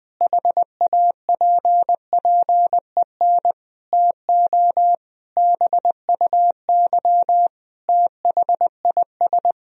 Morse code, heard as a keyed tone, about 20 wpm, 700 Hz. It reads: HAPPEN TO BUY THIS
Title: Sentences from Top 1000 Words - Review Part 4 - 20wpm